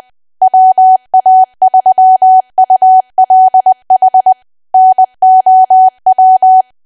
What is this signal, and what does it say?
Morse code, heard as a keyed tone, about 20 wpm, 750 Hz. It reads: WA3ULH NOW
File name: cwa / W1AW-Level3 / 20WPM